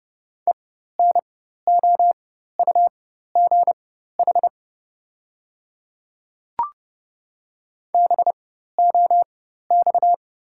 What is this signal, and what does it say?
Morse code, heard as a keyed tone, about 30 wpm, spaced as for 14 wpm, 690 Hz. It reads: ENOUGH BOX
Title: 4x Character Spacing - Top 500 Words - 30wpm